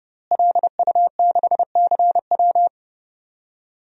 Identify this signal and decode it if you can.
Morse code, heard as a keyed tone, about 30 wpm, 700 Hz. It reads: LU6CW
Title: DXCC-100 Call Signs - 5 Characters - Repeated 2x - 30wpm